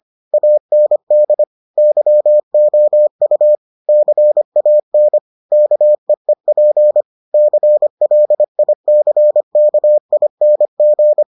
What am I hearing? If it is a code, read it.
Morse code, heard as a keyed tone, about 25 wpm, 600 Hz. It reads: AND YOU CAN KEEP CLICKING